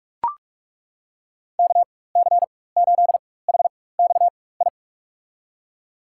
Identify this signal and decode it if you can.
Morse code, heard as a keyed tone, about 42 wpm, spaced as for 20 wpm, 700 Hz. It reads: KC8HXI
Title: USA Call Signs - 6 Characters - 4x Character Spacing - 45wpm